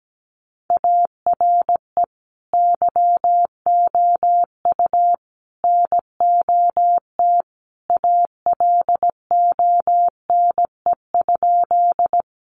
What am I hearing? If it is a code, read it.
Morse code, heard as a keyed tone, about 17 wpm, 700 Hz. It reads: ARE YOU NOT ALONE?
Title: Sentences from Top 1000 Words 17wpm